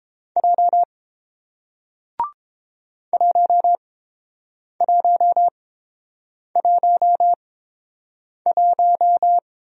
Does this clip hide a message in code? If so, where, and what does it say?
Yes: Morse code from 0.4 to 9.4 s, heard as a keyed tone, about 30 wpm, spaced as for 7 wpm, 700 Hz. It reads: J 1111